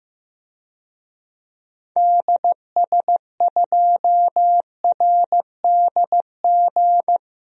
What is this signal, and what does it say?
Morse code, heard as a keyed tone, about 15 wpm, 700 Hz. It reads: DS2RDG
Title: DXCC-100 Call Signs - 6 Characters - 15wpm